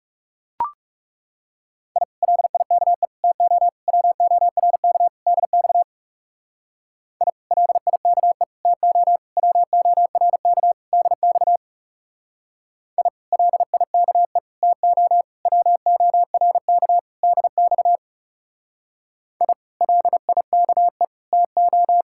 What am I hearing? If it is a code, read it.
Morse code, heard as a keyed tone, about 36 wpm, 700 Hz. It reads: I LIKE TO WORK DX I LIKE TO WORK DX I LIKE TO WORK DX I LIKE TO
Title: Speed-Racing Common QSO Phrases - 30wpm